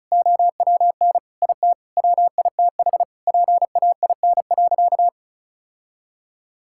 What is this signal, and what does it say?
Morse code, heard as a keyed tone, about 34 wpm, 700 Hz. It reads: OWN IT WITH PAIN.
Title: Sentences from Top 1600 Words 35wpm